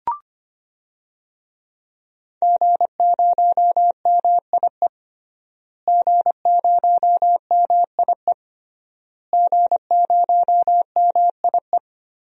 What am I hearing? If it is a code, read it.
Morse code, heard as a keyed tone, about 25 wpm, 700 Hz. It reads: G0MIE G0MIE G0MIE
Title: DXCC-100 Call Signs - 5 Characters - Repeated 3x - 25wpm